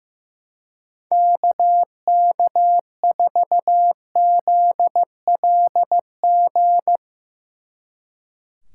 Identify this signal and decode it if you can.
Morse code, heard as a keyed tone, about 17 wpm, 700 Hz. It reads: KK4ZLG
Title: Speed Racing - USA 2x3 Call Signs 15wpm